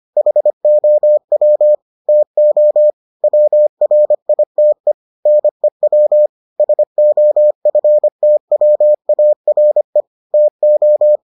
Morse code, heard as a keyed tone, about 25 wpm, 600 Hz. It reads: HOW TO WRITE NEW SOFTWARE TO